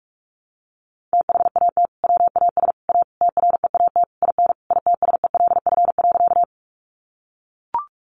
Morse code, heard as a keyed tone, about 44 wpm, 700 Hz. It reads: THAT WAS A TREAT IN ITSELF.
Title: Sentences from Top 1300 Words 45wpm